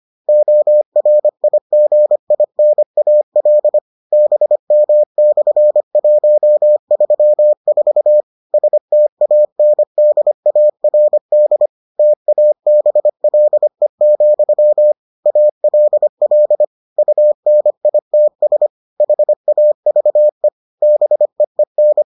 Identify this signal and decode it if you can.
Morse code, heard as a keyed tone, about 25 wpm, 600 Hz. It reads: ORIGINAL BM/134 STANDARD TABLE, ALL UNITS HAVE BEEN